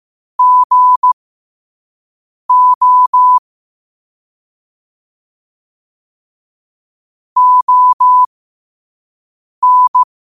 Morse code, heard as a keyed tone, about 15 wpm, spaced as for 5 wpm, 1000 Hz. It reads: GO ON
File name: qotdfiles / 5wpm